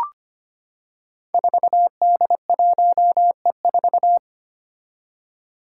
Morse code, heard as a keyed tone, about 25 wpm, 700 Hz. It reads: 4D1E4
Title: Five Challenging Letter-Number Combinations 25wpm